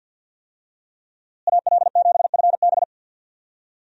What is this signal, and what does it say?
Morse code, heard as a keyed tone, about 50 wpm, 700 Hz. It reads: AL7FB
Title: USA Call Signs - 5 Characters - Repeated 2x - 50wpm